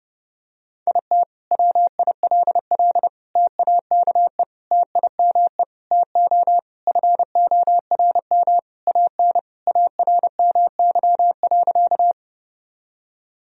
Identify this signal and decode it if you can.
Morse code, heard as a keyed tone, about 30 wpm, 700 Hz. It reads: IT WILL TAKE TIME TO FORM AN ARMY.
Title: Sentences from Top 1000 Words - Review Part 1 - 30wpm